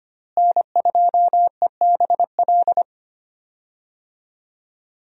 Morse code, heard as a keyed tone, about 25 wpm, 700 Hz. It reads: N2EBL
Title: USA Call Signs - 5 Characters - Repeated 2x - 25wpm